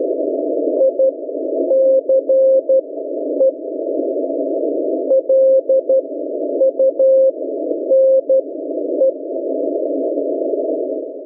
Morse code, heard as a keyed tone, about 12 wpm, spaced as for 9 wpm, 530 Hz. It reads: ICE LUNE